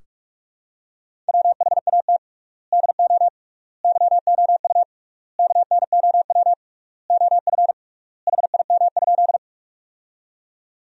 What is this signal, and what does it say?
Morse code, heard as a keyed tone, about 46 wpm, 690 Hz. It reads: WHAT DO YOU KNOW OF HIM?
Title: 3x Word-Spacing - Sentences from Top 100 Words - 45wpm